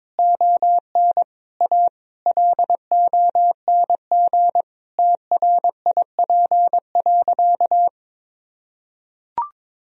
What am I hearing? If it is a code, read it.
Morse code, heard as a keyed tone, about 22 wpm, 700 Hz. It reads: ON A LONG TRIP.
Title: Sentences from Top 1000 Words - Review Part 4 - 22wpm